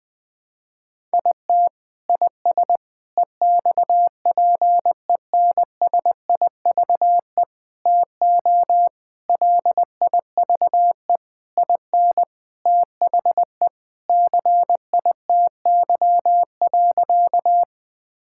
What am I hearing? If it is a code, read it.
Morse code, heard as a keyed tone, about 21 wpm, 700 Hz. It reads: IT IS EXPENSIVE TO LIVE IN THE CITY.
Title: Sentences from Top 1000 Words - Review Part 6 - 20wpm